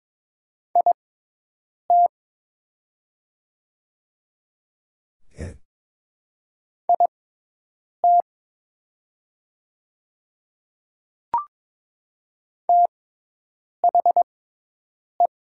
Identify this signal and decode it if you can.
Morse code, heard as a keyed tone, about 22 wpm, spaced as for 8 wpm, 700 Hz. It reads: IT IT THE
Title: 6x Character Spacing - Top 100 Words - 22wpm